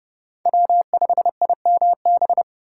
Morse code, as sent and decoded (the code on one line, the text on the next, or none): .-- ..... .. -- -...
W5IMB